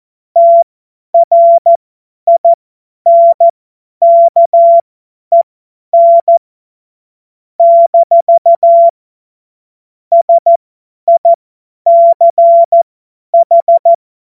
Morse code, heard as a keyed tone, about 14 wpm, spaced as for 10 wpm, 680 Hz. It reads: TRINKEN - SICH
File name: Morsekurs / Witze / Witz 23. Dezember 2025 50 F70